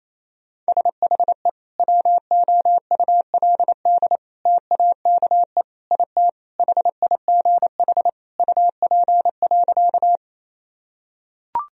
Morse code, heard as a keyed tone, about 28 wpm, 700 Hz. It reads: SHE WOULD TAKE IT HIGH UP.
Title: Sentences from Top 1000 Words - Review Part 1 - 28wpm